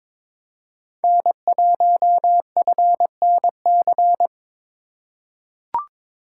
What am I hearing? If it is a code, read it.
Morse code, heard as a keyed tone, about 22 wpm, 700 Hz. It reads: N1FNC
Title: USA Call Signs - 5 Characters - 3x Character Spacing - 22wpm